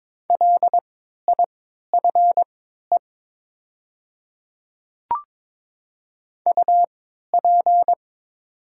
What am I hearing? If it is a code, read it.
Morse code, heard as a keyed tone, about 22 wpm, spaced as for 12 wpm, 700 Hz. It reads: LIFE UP
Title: 3x Character Spacing - Top 300 Words - 22wpm